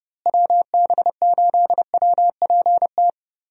Morse code, heard as a keyed tone, about 30 wpm, 700 Hz. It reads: WB8WPT
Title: USA Call Signs - 6 Characters - Repeated 3x - 30wpm